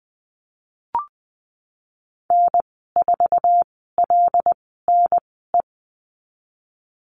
Morse code, heard as a keyed tone, about 19 wpm, spaced as for 14 wpm, 700 Hz. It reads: N4LNE
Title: 179 Lesson 29g - Slash - Call Signs 20wpm